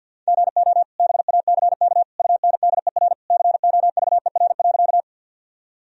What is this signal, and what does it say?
Morse code, heard as a keyed tone, about 51 wpm, 700 Hz. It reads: GO BACK UNDER COVER.
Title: Sentences from Top 1000 Words - Review Part 1 - 50wpm